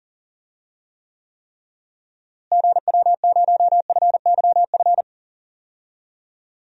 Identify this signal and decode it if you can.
Morse code, heard as a keyed tone, about 40 wpm, 700 Hz. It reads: GW0FYF
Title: DXCC-100 Call Signs - 6 Characters - Repeated 3x - 40wpm